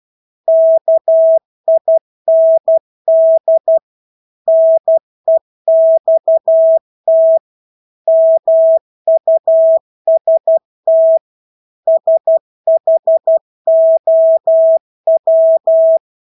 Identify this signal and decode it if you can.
Morse code, heard as a keyed tone, about 12 wpm, 650 Hz. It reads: KIND NEXT MUST SHOW
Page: Engelske ord 12 wpm